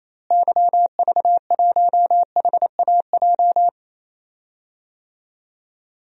Morse code, heard as a keyed tone, about 28 wpm, 700 Hz. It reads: YV1HAJ
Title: DXCC-100 Call Signs - 6 Characters - Repeated 2x - 28wpm